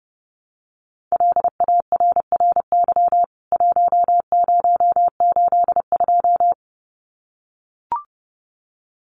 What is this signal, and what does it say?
Morse code, heard as a keyed tone, about 28 wpm, 700 Hz. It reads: LARRY 1082
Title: CWT Exchange 30wpm